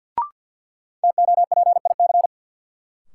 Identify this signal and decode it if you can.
Morse code, heard as a keyed tone, about 51 wpm, 700 Hz. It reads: TOPIC
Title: Sentences from Top 1200 Words - New Word Drill Alt - 50wpm